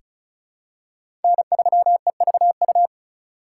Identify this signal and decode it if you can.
Morse code, heard as a keyed tone, about 36 wpm, 700 Hz. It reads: N3EVU
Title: USA Call Signs - 5 Characters - 4x Character Spacing - 35wpm